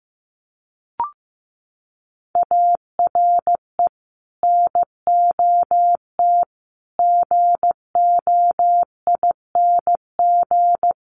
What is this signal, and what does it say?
Morse code, heard as a keyed tone, about 15 wpm, 700 Hz. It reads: ARE NOT GOING
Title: Sets of 5 Words Encore 15wpm